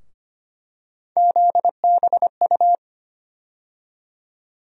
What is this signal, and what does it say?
Morse code, heard as a keyed tone, about 25 wpm, 700 Hz. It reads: ZBU
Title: USA Call Sign Postfixes - 3 Letters 25wpm